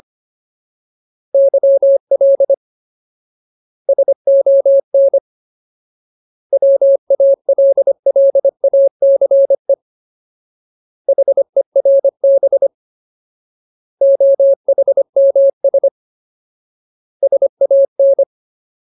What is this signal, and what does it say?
Morse code, heard as a keyed tone, about 25 wpm, 550 Hz. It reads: YL SON WALLACE HERB OHMS SAN
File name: Ultimate Word List / 25WPM Ultimate